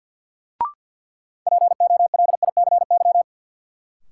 Morse code, heard as a keyed tone, about 48 wpm, 700 Hz. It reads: POLICY
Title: Sentences from Top 1000 Words - Review Part 5 - New Word Drill - 50wpm